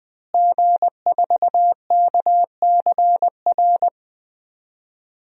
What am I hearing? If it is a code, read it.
Morse code, heard as a keyed tone, about 20 wpm, 700 Hz. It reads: G4KCR
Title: DXCC-100 Call Signs - 5 Characters - Repeated 2x - 20wpm